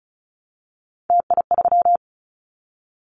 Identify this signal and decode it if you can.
Morse code, heard as a keyed tone, about 36 wpm, 700 Hz. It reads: TI3
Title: Callsign Prefixes for Top 100 DXCC - 35wpm